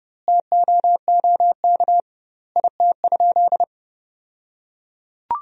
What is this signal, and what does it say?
Morse code, heard as a keyed tone, about 30 wpm, 700 Hz. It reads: TOOK IT?
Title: 2x Word-Spacing - Sentences from Top 300 Words - 30wpm